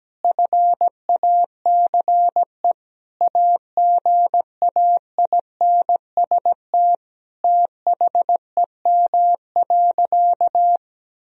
Morse code, heard as a keyed tone, about 17 wpm, 700 Hz. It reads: FACE AGAINST THEM.
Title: Sentences from Top 1000 Words - Review Part 2 - 17wpm